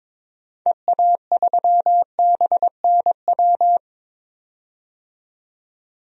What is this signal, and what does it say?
Morse code, heard as a keyed tone, about 21 wpm, 700 Hz. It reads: EA3BNW
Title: DXCC-100 Call Signs - 6 Characters - 22wpm